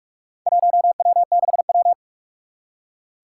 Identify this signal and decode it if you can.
Morse code, heard as a keyed tone, about 44 wpm, 700 Hz. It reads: 1W6W
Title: Four Challenging Letter-Number Combinations 45wpm